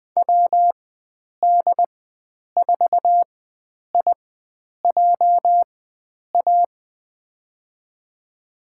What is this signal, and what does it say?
Morse code, heard as a keyed tone, about 20 wpm, spaced as for 9 wpm, 700 Hz. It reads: WD4IJA
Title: USA Call Signs - 6 Characters - 4x Character Spacing - 20wpm